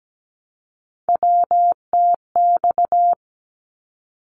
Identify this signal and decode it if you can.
Morse code, heard as a keyed tone, about 17 wpm, 700 Hz. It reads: WTX
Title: ARRL Sections 17wpm